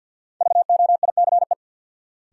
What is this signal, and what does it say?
Morse code, heard as a keyed tone, about 50 wpm, 700 Hz. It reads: VOICE
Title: Sentences from Top 500 Words - New Word Drill - 50wpm